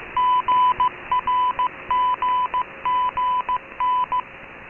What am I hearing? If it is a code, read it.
Morse code, heard as a keyed tone, about 15 wpm, 1000 Hz. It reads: GRGGN